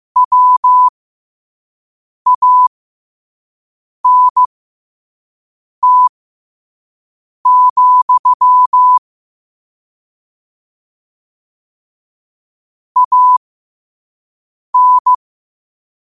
Morse code, heard as a keyed tone, about 15 wpm, spaced as for 5 wpm, 1000 Hz. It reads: WANT, AN
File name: qotdfiles / 5wpm